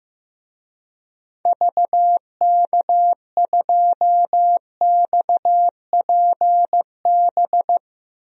Morse code, heard as a keyed tone, about 15 wpm, 700 Hz. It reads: VK2XPB
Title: DXCC-100 Call Signs - 6 Characters - 15wpm